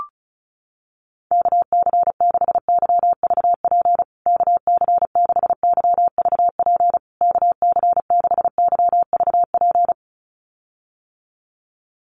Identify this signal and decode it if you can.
Morse code, heard as a keyed tone, about 34 wpm, 700 Hz. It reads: KC6YVP KC6YVP KC6YVP
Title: USA Call Signs Repeated 3x 35wpm